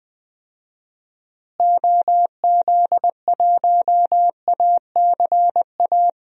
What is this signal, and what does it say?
Morse code, heard as a keyed tone, about 20 wpm, 700 Hz. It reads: OZ1ACA
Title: DXCC-100 Call Signs - 6 Characters - Repeated 3x - 20wpm